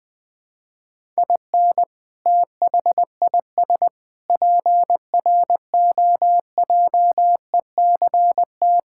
Morse code, heard as a keyed tone, about 21 wpm, 700 Hz. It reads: IN THIS PROJECT
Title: Sentences with 8-Word Phrases - 20wpm